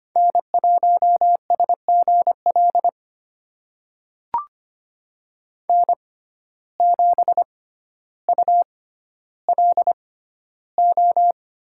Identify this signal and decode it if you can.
Morse code, heard as a keyed tone, about 25 wpm, spaced as for 13 wpm, 700 Hz. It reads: N1SGL N7ULO
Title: USA Call Signs - 5 Characters - 6x Character Spacing - 25wpm